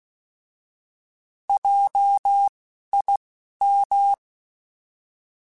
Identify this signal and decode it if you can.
Morse code, heard as a keyed tone, about 16 wpm, spaced as for 11 wpm, 800 Hz. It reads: JIM
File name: Alphabetical / Alphabetical mp3s / Js @10